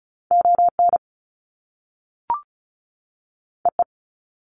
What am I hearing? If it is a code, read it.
Morse code, heard as a keyed tone, about 36 wpm, 700 Hz. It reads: ON EE
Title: Top Two Letter Combinations in Words 35wpm